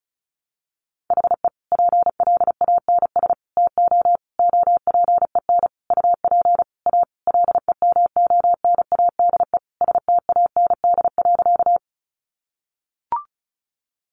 Morse code, heard as a keyed tone, about 34 wpm, 700 Hz. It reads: HE PLANS TO OPEN UP A LEMONADE STAND.
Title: Additional Easy Phrases 35wpm